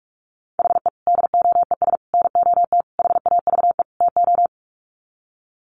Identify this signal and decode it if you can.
Morse code, heard as a keyed tone, about 44 wpm, 700 Hz. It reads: HE DOES NOT HAVE TO
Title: Sets of 5 Words Encore 45wpm